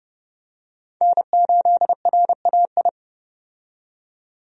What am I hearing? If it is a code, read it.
Morse code, heard as a keyed tone, about 32 wpm, 700 Hz. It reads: N8RAI
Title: USA Call Signs - 5 Characters - 30wpm